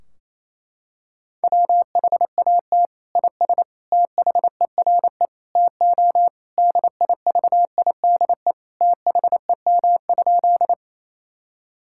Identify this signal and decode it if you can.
Morse code, heard as a keyed tone, about 28 wpm, 700 Hz. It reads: WHAT IS THERE TO DIVIDE THEM?
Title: Sentences from Top 1600 Words 28wpm